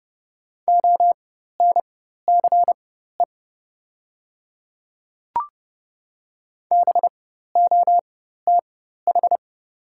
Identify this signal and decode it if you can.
Morse code, heard as a keyed tone, about 30 wpm, spaced as for 14 wpm, 700 Hz. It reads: ONCE BOTH
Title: 4x Character Spacing - Top 300 Words - 30wpm